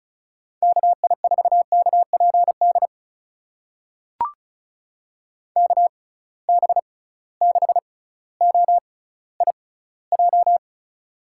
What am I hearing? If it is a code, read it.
Morse code, heard as a keyed tone, about 36 wpm, spaced as for 18 wpm, 700 Hz. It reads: KI4KPD KB6OIJ
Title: USA Call Signs - 6x Character Spacing - 35wpm